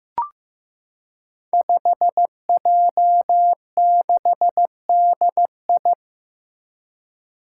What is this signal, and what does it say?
Morse code, heard as a keyed tone, about 15 wpm, 700 Hz. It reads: 5J6DI
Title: Five Challenging Letter-Number Combinations 15wpm